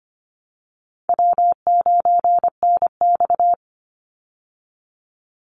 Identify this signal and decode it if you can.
Morse code, heard as a keyed tone, about 25 wpm, 700 Hz. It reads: W9NX
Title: Easy Call Signs Repeated 3x - 25wpm